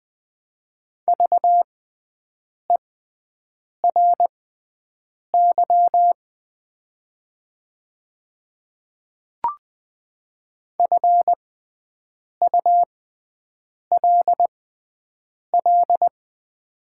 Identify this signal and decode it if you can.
Morse code, heard as a keyed tone, about 20 wpm, spaced as for 7 wpm, 700 Hz. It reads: VERY FULL
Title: 6x Character Spacing - Top 500 Words - 20wpm